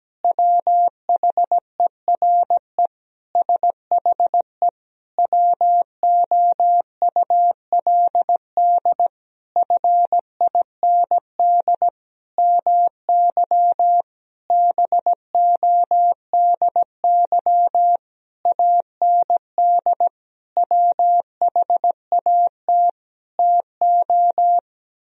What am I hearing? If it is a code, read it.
Morse code, heard as a keyed tone, about 17 wpm, 700 Hz. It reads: WHERE SHE WOULD FIND MY BODY AND WHAT TO
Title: Sentences from Top 1000 Words - Review Part 2 - 17wpm